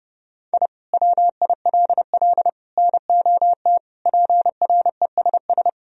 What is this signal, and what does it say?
Morse code, heard as a keyed tone, about 30 wpm, 700 Hz. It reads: I WILL NOT PRESS
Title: Sentences from Top 1000 Words - Review Part 1 - 30wpm